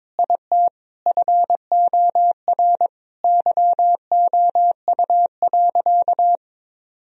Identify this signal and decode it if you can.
Morse code, heard as a keyed tone, about 22 wpm, 700 Hz. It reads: IT FOR YOU.
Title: Sentences from Top 1000 Words - Review Part 2 - 22wpm